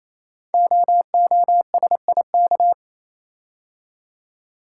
Morse code, heard as a keyed tone, about 27 wpm, 700 Hz. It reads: OOSIK
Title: Code Groups - 5 Letters - 28wpm